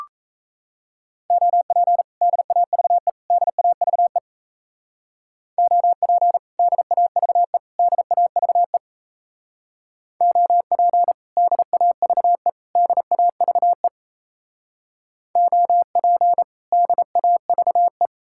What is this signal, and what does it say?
Morse code, heard as a keyed tone, about 37 wpm, 700 Hz. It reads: OP DAVE DAVE OP DAVE DAVE OP DAVE DAVE OP DAVE